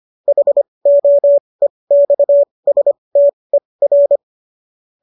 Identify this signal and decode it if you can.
Morse code, heard as a keyed tone, about 25 wpm, spaced as for 20 wpm, 570 Hz. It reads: HOEXSTER
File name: qso219 20